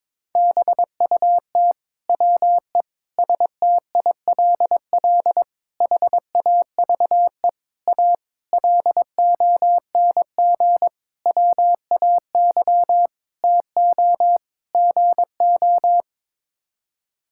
Morse code, heard as a keyed tone, about 22 wpm, 700 Hz. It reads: BUT WE STILL HAVE A LONG WAY TO GO
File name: Sets of 9 Words - 22wpm